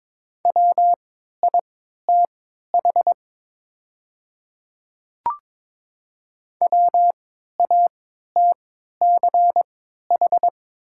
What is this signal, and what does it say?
Morse code, heard as a keyed tone, about 22 wpm, spaced as for 12 wpm, 700 Hz. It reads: WITH WATCH
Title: 3x Character Spacing - Top 300 Words - 22wpm